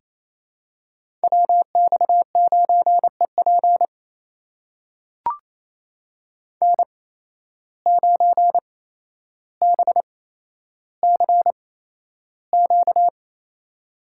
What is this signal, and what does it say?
Morse code, heard as a keyed tone, about 28 wpm, spaced as for 12 wpm, 700 Hz. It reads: WX9EP N9BCQ
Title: USA Call Signs - 5 Characters - 8x Character Spacing - 28wpm